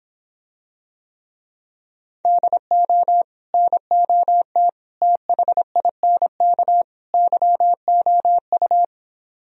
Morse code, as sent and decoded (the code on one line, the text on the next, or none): -.. --- / -. --- - / - .... .. -. -.- / -.-- --- ..-
DO NOT THINK YOU